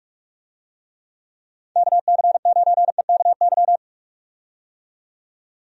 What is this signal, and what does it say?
Morse code, heard as a keyed tone, about 46 wpm, 700 Hz. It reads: KC9EKY
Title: USA Call Signs - 6 Characters - 45wpm